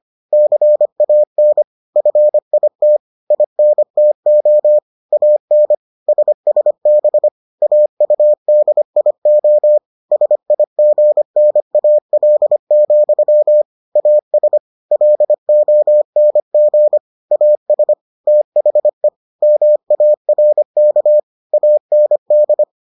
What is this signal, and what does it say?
Morse code, heard as a keyed tone, about 25 wpm, 600 Hz. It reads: CAN FIT INTO AN SSB AUDIO SIGNAL, AS LONG AS THE MARK AND